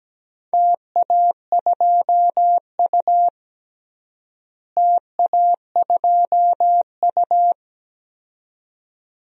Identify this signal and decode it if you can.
Morse code, heard as a keyed tone, about 17 wpm, 700 Hz. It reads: TA2U TA2U
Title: DXCC-100 Call Signs - 4 Characters - Repeated 2x - 17wpm